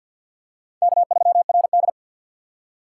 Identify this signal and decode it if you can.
Morse code, heard as a keyed tone, about 53 wpm, 700 Hz. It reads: K3RD
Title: USA Call Signs - 4 Characters - 2x Character Spacing - 50wpm